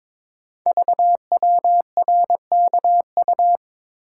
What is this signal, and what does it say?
Morse code, heard as a keyed tone, about 22 wpm, 700 Hz. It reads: VWRKU
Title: Code Groups - 5 Letter:Numbers - 22wpm